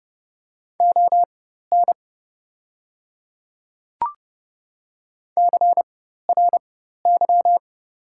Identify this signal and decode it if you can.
Morse code, heard as a keyed tone, about 30 wpm, spaced as for 14 wpm, 700 Hz. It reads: ON CRY
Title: 4x Character Spacing - Top 500 Words - 30wpm